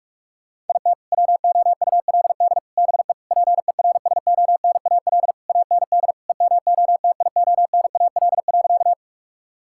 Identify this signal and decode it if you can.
Morse code, heard as a keyed tone, about 42 wpm, 700 Hz. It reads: IT WOULD BE PERSONAL AND EMOTIONAL.
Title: Sentences from Top 1000 Words - Review Part 5 - 45wpm